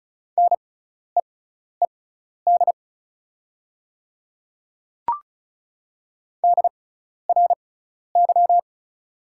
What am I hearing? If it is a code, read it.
Morse code, heard as a keyed tone, about 34 wpm, spaced as for 12 wpm, 690 Hz. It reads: NEED DRY
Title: 6x Character Spacing - Top 500 Words - 35wpm